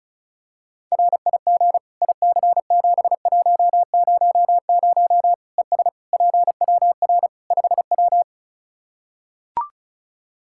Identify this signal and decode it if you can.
Morse code, heard as a keyed tone, about 36 wpm, 700 Hz. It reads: RIG IC7100 ES PWR 5W
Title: Practice QSO - Exchange 2 - Info - 35wpm